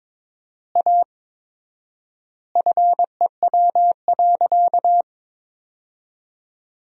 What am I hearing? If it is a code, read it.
Morse code, heard as a keyed tone, about 22 wpm, 700 Hz. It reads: A FEW.
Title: 4x Word-Spacing - Sentences from Top 300 Words - 22wpm